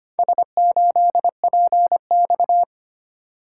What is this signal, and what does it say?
Morse code, heard as a keyed tone, about 25 wpm, 700 Hz. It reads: S8PX
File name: practice / 25wpm / Four Letter-Number Combinations - 25wpm